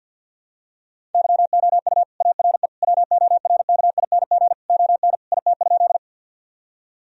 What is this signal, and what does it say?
Morse code, heard as a keyed tone, about 50 wpm, 700 Hz. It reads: YOU ARE WORKING ON IT?
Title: Sentences from Top 1000 Words - Review Part 4 - 50wpm